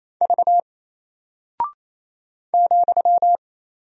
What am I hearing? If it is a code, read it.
Morse code, heard as a keyed tone, about 27 wpm, 700 Hz. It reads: V ,
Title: Single Character (Letter:Number:Punctuation:Prosign) 28wpm